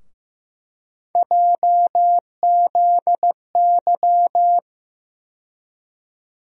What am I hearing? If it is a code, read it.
Morse code, heard as a keyed tone, about 15 wpm, 700 Hz. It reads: JZY
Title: USA Call Sign Postfixes - 3 Letters 15wpm